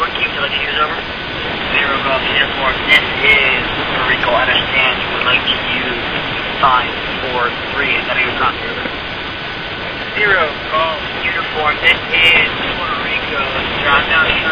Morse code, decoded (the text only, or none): none